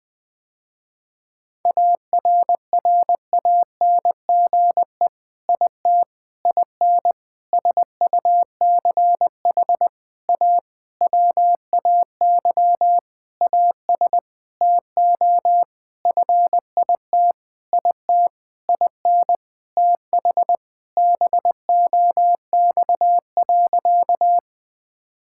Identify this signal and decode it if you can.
Morse code, heard as a keyed tone, about 21 wpm, 700 Hz. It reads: ARRANGE IT IN SUCH A WAY AS TO FIT IT IN TH BOX.
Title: Sentences with 6-Word Phrases - Spoken Beginning - 20wpm